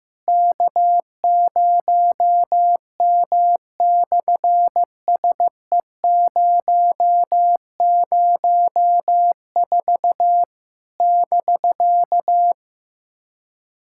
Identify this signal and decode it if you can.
Morse code, heard as a keyed tone, about 15 wpm, 700 Hz. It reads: K0M/SE004 <BK>
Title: SOTA QSO Practice 15wpm